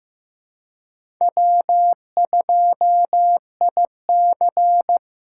Morse code, heard as a keyed tone, about 18 wpm, 700 Hz. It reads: W2IC
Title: Speed Racing - USA Call Signs 15wpm